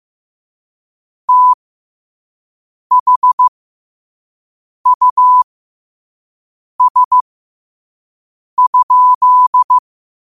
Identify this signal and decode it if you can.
Morse code, heard as a keyed tone, about 15 wpm, spaced as for 5 wpm, 1000 Hz. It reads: THUS?
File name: qotdfiles / 5wpm